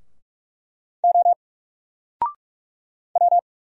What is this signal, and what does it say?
Morse code, heard as a keyed tone, about 45 wpm, 700 Hz. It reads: O W
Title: Single Letter-Number 45wpm